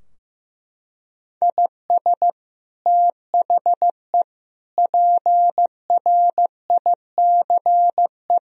Morse code, heard as a keyed tone, about 15 wpm, 700 Hz. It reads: IS THE PRICE
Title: Sentences from Top 1000 Words - Review Part 4 - 15wpm